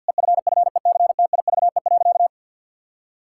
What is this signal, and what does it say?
Morse code, heard as a keyed tone, about 48 wpm, 700 Hz. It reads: EFFECTIVE.
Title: Sentences from Top 1000 Words - Review Part 6 - 50wpm